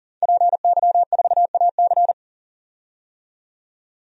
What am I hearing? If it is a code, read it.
Morse code, heard as a keyed tone, about 40 wpm, 700 Hz. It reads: PY4AC
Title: DXCC-100 Call Signs - 5 Characters - 40wpm